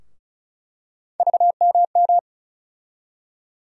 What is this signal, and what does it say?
Morse code, heard as a keyed tone, about 34 wpm, 700 Hz. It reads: VMM
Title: USA Call Sign Postfixes - 3 Letters 35wpm